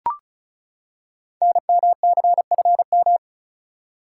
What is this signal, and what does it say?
Morse code, heard as a keyed tone, about 36 wpm, 700 Hz. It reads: NMCFM
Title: Code Groups - 5 Letters - 35wpm